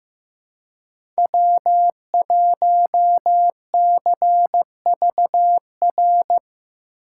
Morse code, heard as a keyed tone, about 15 wpm, 700 Hz. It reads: W1CVR
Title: USA Call Signs - 5 Characters - 6x Character Spacing - 15wpm